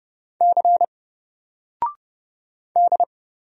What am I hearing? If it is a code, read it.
Morse code, heard as a keyed tone, about 31 wpm, 700 Hz. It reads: C D
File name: Single-Letter-Number-30wpm